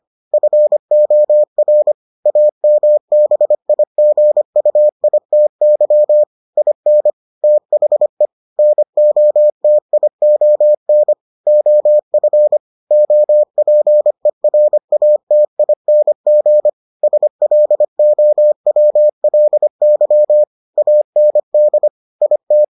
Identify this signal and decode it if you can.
Morse code, heard as a keyed tone, about 25 wpm, 600 Hz. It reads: FOR AMBIGUITY IN THE NOTION OF OPERATING SLOWLY AND IT